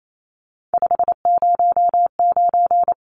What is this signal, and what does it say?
Morse code, heard as a keyed tone, about 28 wpm, 700 Hz. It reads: 509
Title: Numbers - 3-Digit 28wpm